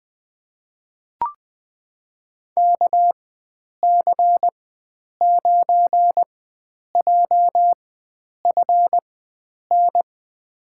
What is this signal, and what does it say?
Morse code, heard as a keyed tone, about 20 wpm, spaced as for 9 wpm, 700 Hz. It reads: KC9JFN
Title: USA Call Signs - 6 Characters - 4x Character Spacing - 20wpm